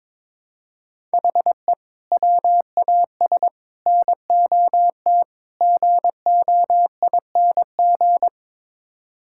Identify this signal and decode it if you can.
Morse code, heard as a keyed tone, about 22 wpm, 700 Hz. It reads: HE WAS NOT GOING
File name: practice / 22wpm / Build A Phrase - 5-Word Phrases - 22wpm